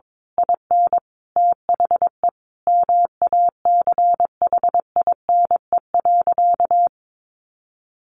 Morse code, heard as a keyed tone, about 22 wpm, 700 Hz. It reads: IN THE MACHINE.
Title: A Sound Of Thunder 22wpm